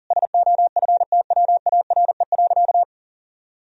Morse code, heard as a keyed tone, about 40 wpm, 700 Hz. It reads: SOFTWARE.
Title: Sentences from Top 1000 Words - Review Part 3 - 40wpm